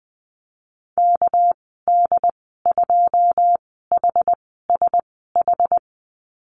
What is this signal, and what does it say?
Morse code, heard as a keyed tone, about 20 wpm, 700 Hz. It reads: K D 2 H S H
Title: 200 Lesson 32g - Letter k - Call Signs 20wpm